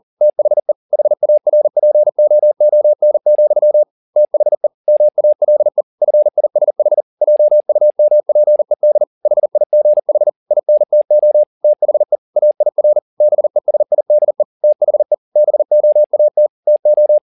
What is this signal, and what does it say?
Morse code, heard as a keyed tone, about 40 wpm, 600 Hz. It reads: THE HARPOON, THE MALE FISH JUMPED HIGH INTO THE AIR BESIDE THE BOAT TO